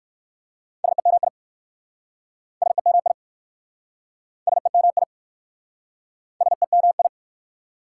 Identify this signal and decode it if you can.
Morse code, heard as a keyed tone, about 62 wpm, 690 Hz. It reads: SEMI SEMI SEMI SEMI